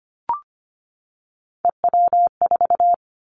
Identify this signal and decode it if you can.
Morse code, heard as a keyed tone, about 25 wpm, 700 Hz. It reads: EW4
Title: Callsign Prefixes for Top 50 DXCC - 25wpm